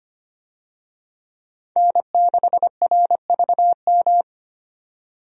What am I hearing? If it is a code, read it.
Morse code, heard as a keyed tone, about 25 wpm, 690 Hz. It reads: N6RVM
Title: USA Call Signs - 5 Characters - 25wpm